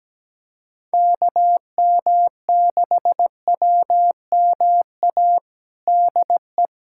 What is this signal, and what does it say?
Morse code, heard as a keyed tone, about 17 wpm, 700 Hz. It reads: KM6WMA DE
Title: USA Callsign DE Callsign - 6 Characters - 17wpm